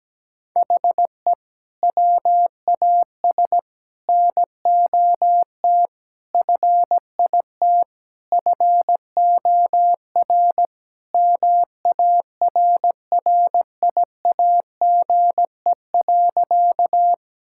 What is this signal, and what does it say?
Morse code, heard as a keyed tone, about 17 wpm, 700 Hz. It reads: HE WAS NOT FIT FOR MARRIAGE.
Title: Sentences from Top 1000 Words - Review Part 5 - 17wpm